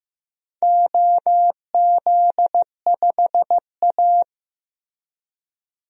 Morse code, heard as a keyed tone, about 15 wpm, 700 Hz. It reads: OZ5A
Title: DXCC-100 Call Signs - 4 Characters - Repeated 3x - 15wpm